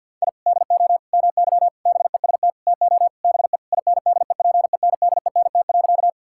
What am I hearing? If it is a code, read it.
Morse code, heard as a keyed tone, about 50 wpm, 700 Hz. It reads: I DO MY BEST TO BE INDEPENDENT.
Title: Sentences from Top 1000 Words - Review Part 6 - 50wpm